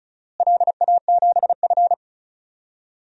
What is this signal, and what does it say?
Morse code, heard as a keyed tone, about 36 wpm, 700 Hz. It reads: LA7F
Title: DXCC-100 Call Signs - 4 Characters - Repeated 2x - 35wpm